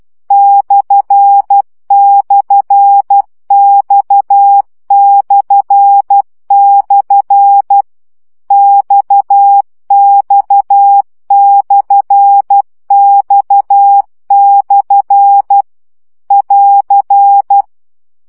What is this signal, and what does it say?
Morse code, heard as a keyed tone, about 12 wpm, 800 Hz. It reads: //X// XX/X/ <AR>
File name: CW gyakorlas x per 10 baud